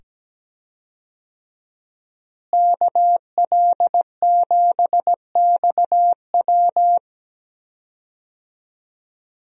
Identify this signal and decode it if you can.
Morse code, heard as a keyed tone, about 17 wpm, 700 Hz. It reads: KL7XW
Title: DXCC-100 Call Signs - 5 Characters - Repeated 2x - 17wpm